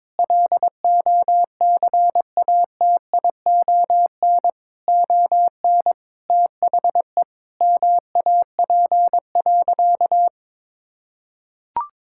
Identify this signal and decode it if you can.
Morse code, heard as a keyed tone, about 22 wpm, 700 Hz. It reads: LOCATION ON THE MAP.
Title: Sentences from Top 1000 Words - Review Part 6 - 22wpm